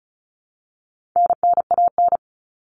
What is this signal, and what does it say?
Morse code, heard as a keyed tone, about 34 wpm, 700 Hz. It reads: NNAN
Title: All Cut Numbers - 4-Digit 35wpm